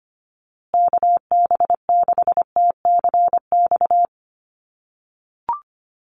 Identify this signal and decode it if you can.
Morse code, heard as a keyed tone, about 25 wpm, 700 Hz. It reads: KB6TCX
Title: USA Call Signs Repeated 2x 25wpm